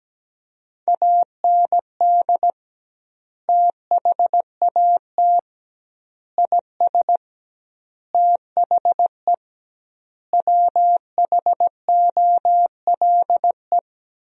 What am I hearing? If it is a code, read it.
Morse code, heard as a keyed tone, about 17 wpm, 700 Hz. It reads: AND THAT IS THE WHOLE
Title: 2x Word-Spacing - Sentences from Top 400 Words - 17wpm